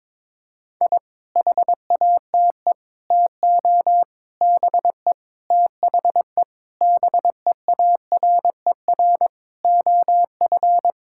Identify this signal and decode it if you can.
Morse code, heard as a keyed tone, about 22 wpm, 700 Hz. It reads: I HATE TO BE THE BEARER OF
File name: Sentences with 9-Word Phrases - 22wpm